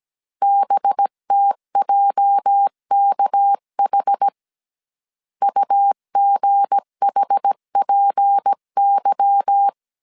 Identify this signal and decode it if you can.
Morse code, heard as a keyed tone, about 17 wpm, 800 Hz. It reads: BTJKH UGHPY
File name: CW / DET / 1 9 1500C S B 16wpm 2856